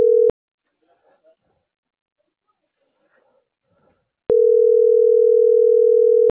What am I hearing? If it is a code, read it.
Ringback tone (North America), 440 + 480 Hz.